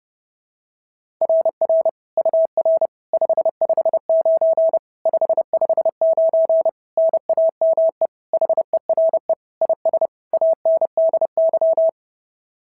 Morse code, heard as a keyed tone, about 30 wpm, 650 Hz. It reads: RR UR 559 559 NAME HERE IS ANDY
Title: SOTA QSO Practice 30wpm